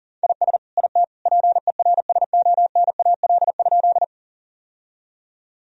Morse code, heard as a keyed tone, about 40 wpm, 700 Hz. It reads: IS IT PERSONAL?
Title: Sentences from Top 1000 Words - Review Part 4 - 40wpm